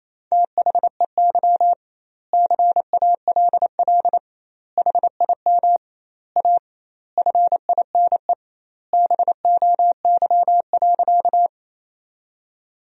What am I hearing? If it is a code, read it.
Morse code, heard as a keyed tone, about 28 wpm, 700 Hz. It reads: THEY CALL HIM A FINE BOY.
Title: 2x Word-Spacing - Sentences from Top 500 Words - 28wpm